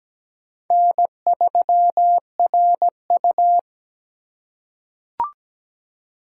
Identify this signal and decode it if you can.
Morse code, heard as a keyed tone, about 17 wpm, 700 Hz. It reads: N3RU
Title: USA Call Signs - 4 Characters - 3x Character Spacing - 17wpm